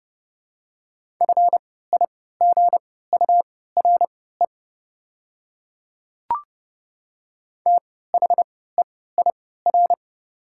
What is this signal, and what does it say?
Morse code, heard as a keyed tone, about 30 wpm, spaced as for 17 wpm, 690 Hz. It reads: FIGURE THEIR
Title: 3x Character Spacing - Top 500 Words - 30wpm